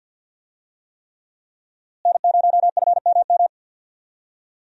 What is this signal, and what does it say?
Morse code, heard as a keyed tone, about 47 wpm, 700 Hz. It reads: N0FMM
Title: USA Call Signs - 5 Characters - Repeated 3x - 50wpm